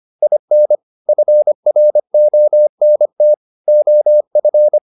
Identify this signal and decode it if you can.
Morse code, heard as a keyed tone, about 25 wpm, 600 Hz. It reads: IN FRONT OF